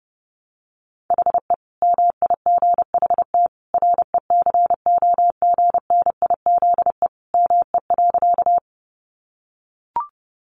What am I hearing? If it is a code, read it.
Morse code, heard as a keyed tone, about 30 wpm, 700 Hz. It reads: HE MIGHT RECOGNIZE ME.
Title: Sentences from Top 1400 Words 30wpm